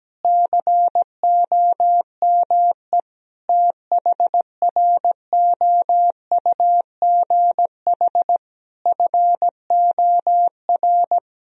Morse code, heard as a keyed tone, about 17 wpm, 700 Hz. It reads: COME THROUGH FOR
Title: Sentences from Top 1000 Words - Review Part 1 - 17wpm